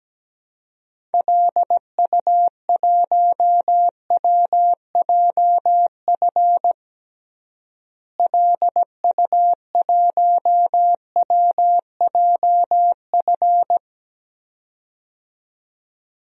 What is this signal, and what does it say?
Morse code, heard as a keyed tone, about 17 wpm, 700 Hz. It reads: LU1WJF LU1WJF
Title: DXCC-100 Call Signs - 6 Characters - Repeated 2x - 17wpm